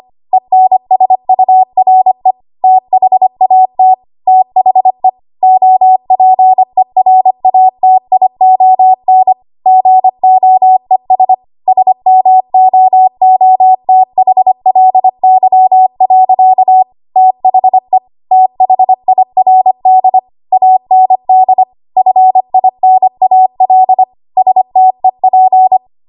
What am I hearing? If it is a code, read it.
Morse code, heard as a keyed tone, about 25 wpm, 750 Hz. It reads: ENSURE THAT THE OPERATION GOES SMOOTHLY. THE THIRD AND FINAL STEP
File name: Archive / 25 WPM / 25WPM